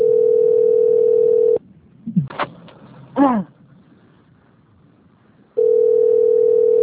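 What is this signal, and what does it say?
Ringback tone (North America), 440 + 480 Hz.
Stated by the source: Call recording from the Limeira 4 - SP phone system.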